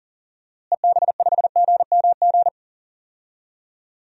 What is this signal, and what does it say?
Morse code, heard as a keyed tone, about 40 wpm, 700 Hz. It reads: EB5GMG
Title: DXCC-100 Call Signs - 6 Characters - Repeated 3x - 40wpm